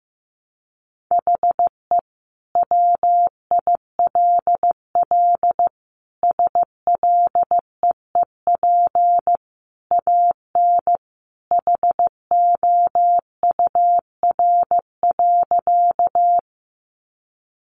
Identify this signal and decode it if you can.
Morse code, heard as a keyed tone, about 15 wpm, 700 Hz. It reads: HE WILL SLEEP AN HOUR.
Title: Sentences from Top 500 Words 15wpm